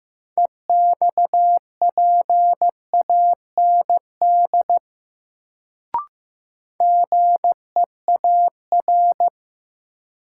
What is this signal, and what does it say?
Morse code, heard as a keyed tone, about 15 wpm, 700 Hz. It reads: EXPAND GEAR
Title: Sentences from Top 1400 Words - New Word Drill - 15wpm